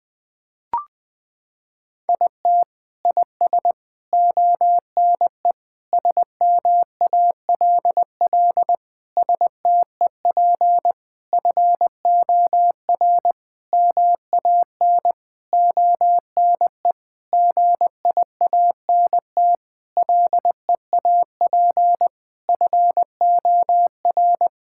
Morse code, heard as a keyed tone, about 20 wpm, 700 Hz. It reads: IT IS ONE SMALL STEP FOR MAN ONE GIANT LEAP FOR
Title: Sentences with 9-Word Phrases - 20wpm